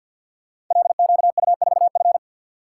Morse code, heard as a keyed tone, about 52 wpm, 700 Hz. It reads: LQU4F
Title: Code Groups - 5 Letter:Numbers - 50wpm